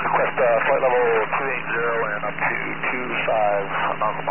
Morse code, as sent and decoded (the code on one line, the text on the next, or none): none